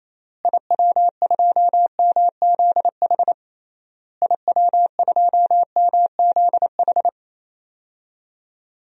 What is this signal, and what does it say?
Morse code, heard as a keyed tone, about 27 wpm, 700 Hz. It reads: IW2MZH IW2MZH
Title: DXCC-100 Call Signs - 6 Characters - Repeated 3x - 28wpm